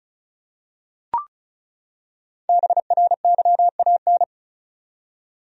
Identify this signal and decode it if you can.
Morse code, heard as a keyed tone, about 36 wpm, 700 Hz. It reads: BRYAN
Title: USA Common Names - Max 5 Letters - 35wpm